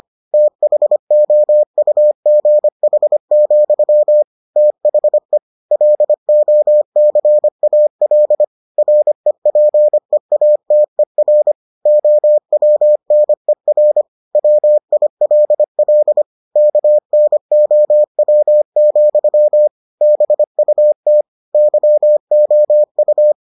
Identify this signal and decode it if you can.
Morse code, heard as a keyed tone, about 25 wpm, 600 Hz. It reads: THOUGH, THE LOCAL REPEATER OWNER WILL KNOW, BUT YOU